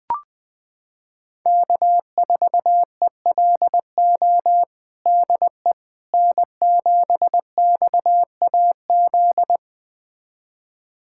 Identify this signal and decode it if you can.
Morse code, heard as a keyed tone, about 20 wpm, 700 Hz. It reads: K4ELO DE N7XAZ
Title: USA Callsign DE Callsign 20wpm